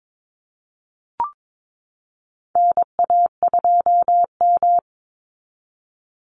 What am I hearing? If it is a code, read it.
Morse code, heard as a keyed tone, about 22 wpm, 690 Hz. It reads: NA2M
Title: Call Signs Repeated 2x - 22wpm